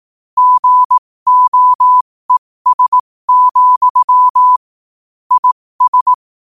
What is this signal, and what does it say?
Morse code, heard as a keyed tone, about 18 wpm, spaced as for 15 wpm, 1000 Hz. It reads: GOES, IS
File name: qotdfiles / 15wpm